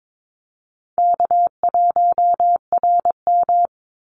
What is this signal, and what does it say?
Morse code, heard as a keyed tone, about 22 wpm, 700 Hz. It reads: K1RM
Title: Easy Call Signs - 22wpm